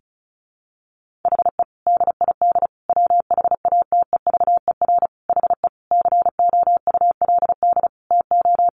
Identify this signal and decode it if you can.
Morse code, heard as a keyed tone, about 34 wpm, 700 Hz. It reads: HE DID WHATEVER HE COULD TO
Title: Sentences from Top 1100 Words 35wpm